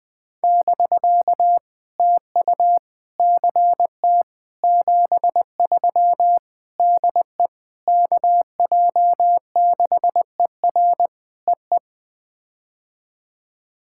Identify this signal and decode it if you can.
Morse code, heard as a keyed tone, about 20 wpm, 700 Hz. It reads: <BK> TU CT 73 DE KJ6ER EE
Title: POTA QSO Practice - 20wpm